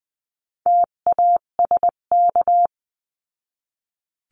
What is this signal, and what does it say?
Morse code, heard as a keyed tone, about 20 wpm, 700 Hz. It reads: TASK
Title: 258 Lesson 42c - Words 16-20wpm